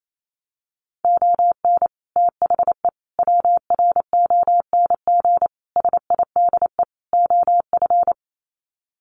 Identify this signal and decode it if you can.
Morse code, heard as a keyed tone, about 28 wpm, 700 Hz. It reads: ON THE WRONG SIDE OF
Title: Sets of 5 Words Encore 28wpm